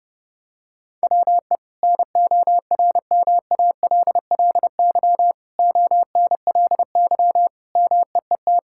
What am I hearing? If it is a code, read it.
Morse code, heard as a keyed tone, about 30 wpm, 700 Hz. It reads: WE NORMALLY ONLY MEET
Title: Sentences from Top 1000 Words - Review Part 6 - 30wpm